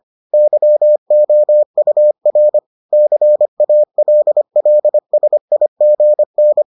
Morse code, heard as a keyed tone, about 25 wpm, 600 Hz. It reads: YOUR CALLSIGN